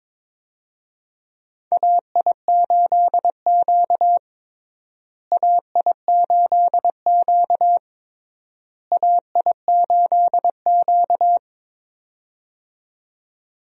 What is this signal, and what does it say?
Morse code, heard as a keyed tone, about 22 wpm, 700 Hz. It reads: AI8Q AI8Q AI8Q
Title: USA Call Signs - 4 Characters - Repeated 3x - 22wpm